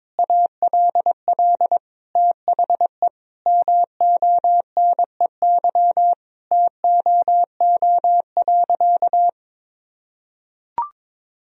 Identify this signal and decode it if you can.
Morse code, heard as a keyed tone, about 22 wpm, 700 Hz. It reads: ALL THE MONEY TOO.
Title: Sentences from Top 1000 Words - Review Part 4 - 22wpm